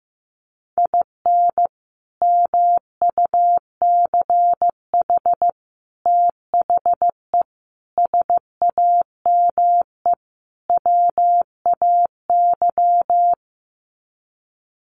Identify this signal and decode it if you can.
Morse code, heard as a keyed tone, about 15 wpm, 700 Hz. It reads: IN MUCH THE SAME WAY
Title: Sets of 5 Words Encore 15wpm